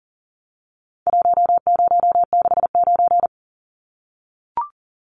Morse code, heard as a keyed tone, about 40 wpm, 700 Hz. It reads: 1069
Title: Numbers - Contest Style - 4-Digit 40wpm